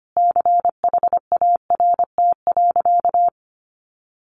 Morse code, heard as a keyed tone, about 25 wpm, 700 Hz. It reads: CHART.
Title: Sentences from Top 1300 Words 25wpm